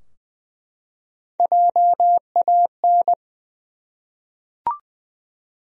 Morse code, heard as a keyed tone, about 20 wpm, 700 Hz. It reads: JAN
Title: USA Common Names - Max 3 Letters - 20wpm